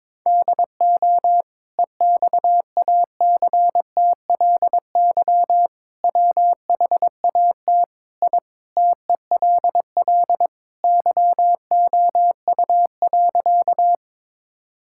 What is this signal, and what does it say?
Morse code, heard as a keyed tone, about 22 wpm, 700 Hz. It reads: DO EXACTLY WHAT I TELL YOU.
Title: Sentences from Top 1000 Words - Review Part 6 - 22wpm